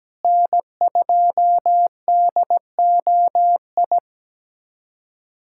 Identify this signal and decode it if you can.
Morse code, heard as a keyed tone, about 17 wpm, 700 Hz. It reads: N2DOI
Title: USA Callsign DE Callsign - 5 Characters - 17wpm